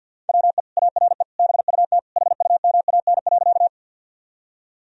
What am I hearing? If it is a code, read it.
Morse code, heard as a keyed tone, about 48 wpm, 700 Hz. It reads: WE ARE BUT HUMAN.
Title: Sentences from Top 1000 Words - Review Part 3 - 50wpm